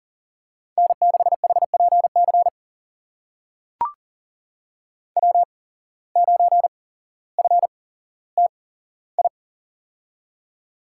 Morse code, heard as a keyed tone, about 39 wpm, spaced as for 17 wpm, 700 Hz. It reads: N6HPC W9FTI
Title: USA Call Signs - 5 Characters - 8x Character Spacing - 40wpm